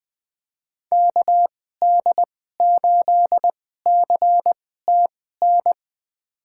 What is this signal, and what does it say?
Morse code, heard as a keyed tone, about 20 wpm, spaced as for 14 wpm, 700 Hz. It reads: KD8CTN
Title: USA Call Signs - 2x Character Spacing - 20wpm